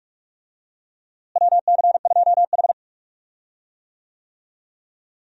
Morse code, heard as a keyed tone, about 41 wpm, 700 Hz. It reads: WC2H
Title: USA Call Signs - 4 Characters - Repeated 2x - 45wpm